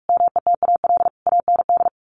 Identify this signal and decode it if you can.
Morse code, heard as a keyed tone, about 45 wpm, 700 Hz. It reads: METAL AND
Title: A Sound Of Thunder 45wpm